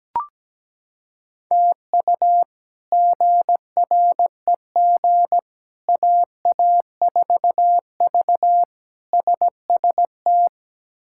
Morse code, heard as a keyed tone, about 17 wpm, 700 Hz. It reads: TU GREG AA4V SST
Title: K1USN SST QSO Practice - 17wpm